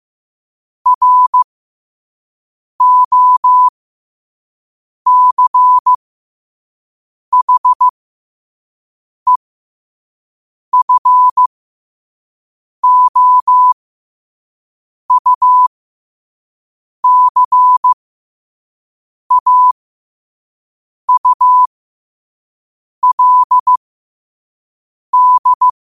Morse code, heard as a keyed tone, about 15 wpm, spaced as for 5 wpm, 1000 Hz. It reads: ROCHEFOUCAULD
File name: qotdfiles / 5wpm